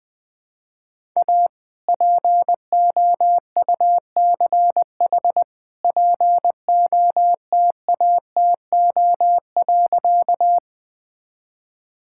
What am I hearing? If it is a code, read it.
Morse code, heard as a keyed tone, about 21 wpm, 700 Hz. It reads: A POUCH POTATO.
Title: All-Age Appropriate Jokes - 20wpm